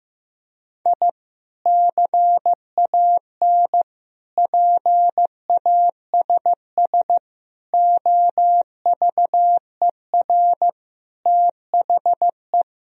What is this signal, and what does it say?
Morse code, heard as a keyed tone, about 15 wpm, 700 Hz. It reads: I CAN PASS OVER THE
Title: Sentences from Top 1000 Words - Review Part 1 - 15wpm